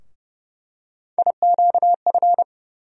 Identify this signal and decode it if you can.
Morse code, heard as a keyed tone, about 30 wpm, 700 Hz. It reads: IQF
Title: USA Call Sign Postfixes - 3 Letters 30wpm